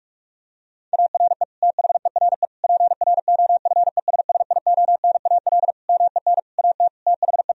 Morse code, heard as a keyed tone, about 43 wpm, 700 Hz. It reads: ARE THERE PROFESSIONAL MEN AT THE
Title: Sentences from Top 1000 Words - Review Part 4 - 45wpm